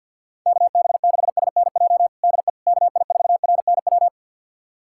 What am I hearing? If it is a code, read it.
Morse code, heard as a keyed tone, about 49 wpm, 700 Hz. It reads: KB6SNJ DE KI4RNW
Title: USA Callsign DE Callsign - 6 Characters - 50wpm